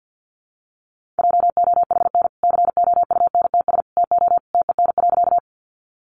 Morse code, heard as a keyed tone, about 48 wpm, 700 Hz. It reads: JOHN COUNTS TO TEN.
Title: Easy Phrases 50wpm